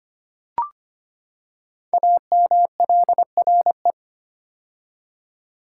Morse code, heard as a keyed tone, about 25 wpm, 700 Hz. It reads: AMLRE
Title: Code Groups - 5 Letters - 25wpm